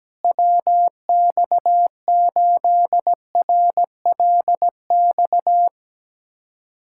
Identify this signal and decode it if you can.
Morse code, heard as a keyed tone, about 17 wpm, 700 Hz. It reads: WX8RLX
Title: USA Callsign DE Callsign 17wpm